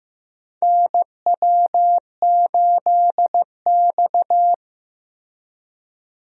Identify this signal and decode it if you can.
Morse code, heard as a keyed tone, about 15 wpm, 700 Hz. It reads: NW8X
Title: USA Call Signs - 4 Characters - Repeated 3x - 15wpm